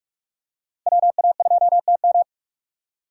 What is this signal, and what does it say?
Morse code, heard as a keyed tone, about 44 wpm, 700 Hz. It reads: WA2TM